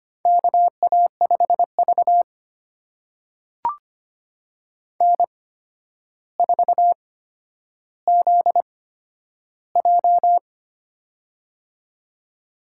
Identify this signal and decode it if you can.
Morse code, heard as a keyed tone, about 25 wpm, spaced as for 11 wpm, 700 Hz. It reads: KA5V N4ZJ
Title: USA Call Signs - 4 Characters - 8x Character Spacing - 25wpm